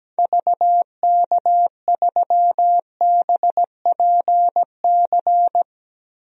Morse code, heard as a keyed tone, about 17 wpm, 700 Hz. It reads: VK3BPC
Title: DXCC-100 Call Signs - 6 Characters - Repeated 3x - 17wpm